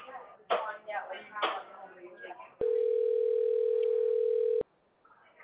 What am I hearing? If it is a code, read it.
Ringback tone (North America), 440 + 480 Hz.